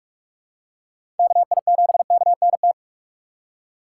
Morse code, heard as a keyed tone, about 46 wpm, 700 Hz. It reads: KI7KNT
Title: USA Call Signs - 6 Characters - 2x Character Spacing - 45wpm